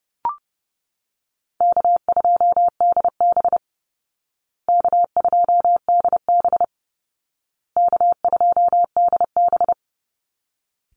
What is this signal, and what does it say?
Morse code, heard as a keyed tone, about 30 wpm, 700 Hz. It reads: K2DB K2DB K2DB
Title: Easy Call Signs Repeated 3x - 30wpm